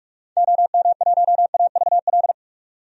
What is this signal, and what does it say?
Morse code, heard as a keyed tone, about 46 wpm, 700 Hz. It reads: OM1AVL
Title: DXCC-100 Call Signs - 6 Characters - 45wpm